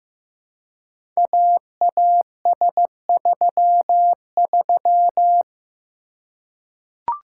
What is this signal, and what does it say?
Morse code, heard as a keyed tone, about 15 wpm, 700 Hz. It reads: AAS33
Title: Code Groups - 5 Letter:Numbers - 15wpm